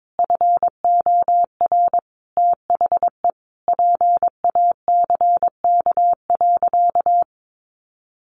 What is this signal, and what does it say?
Morse code, heard as a keyed tone, about 22 wpm, 700 Hz. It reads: FOR THE PACK.
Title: Sentences from Top 1300 Words 22wpm